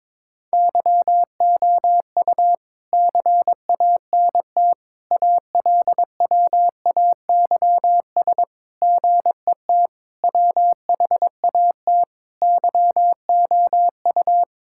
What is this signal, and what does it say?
Morse code, heard as a keyed tone, about 22 wpm, 700 Hz. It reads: YOU CANT ALWAYS GET WHAT YOU
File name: practice / 22wpm / Sets of 7 Words - 22wpm